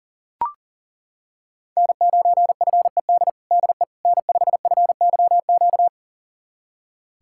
Morse code, heard as a keyed tone, about 37 wpm, 700 Hz. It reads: N9FED DE N5FYQ
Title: USA Callsign DE Callsign - 5 Characters - 40wpm